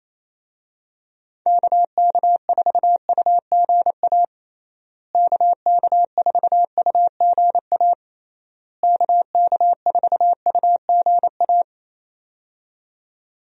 Morse code, heard as a keyed tone, about 27 wpm, 700 Hz. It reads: KK4UGA KK4UGA KK4UGA
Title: USA Call Signs - 6 Characters - Repeated 3x - 28wpm